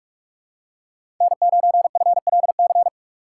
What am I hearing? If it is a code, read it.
Morse code, heard as a keyed tone, about 47 wpm, 700 Hz. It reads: N9FLC
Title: USA Call Signs - 5 Characters - 4x Character Spacing - 45wpm